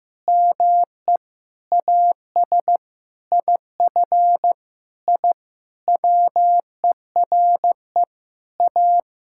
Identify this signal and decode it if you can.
Morse code, heard as a keyed tone, about 15 wpm, 700 Hz. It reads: ME AS IF I WERE A
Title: Sentences from Top 1000 Words - Review Part 6 - 15wpm